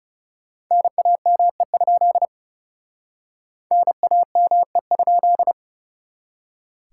Morse code, heard as a keyed tone, about 36 wpm, 700 Hz. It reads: NAME? NAME?
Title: Speed-Racing Common QSO Phrases Encore - 30wpm